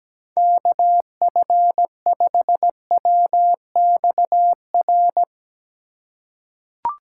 Morse code, heard as a keyed tone, about 17 wpm, 700 Hz. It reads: KF5WXR
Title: USA Call Signs - 3x Character Spacing - 17wpm